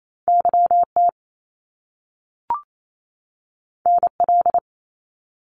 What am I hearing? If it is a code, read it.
Morse code, heard as a keyed tone, about 28 wpm, 700 Hz. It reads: YT NL
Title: Canadian Provinces and Territories 28wpm